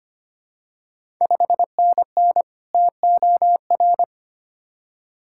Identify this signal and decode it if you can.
Morse code, heard as a keyed tone, about 25 wpm, 700 Hz. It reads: 5NN TOR
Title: Ontario QSO Party Exchange 25wpm